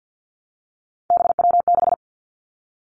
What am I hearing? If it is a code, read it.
Morse code, heard as a keyed tone, about 50 wpm, 700 Hz. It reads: BW6
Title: Three Challenging Letter-Number Combinations 50wpm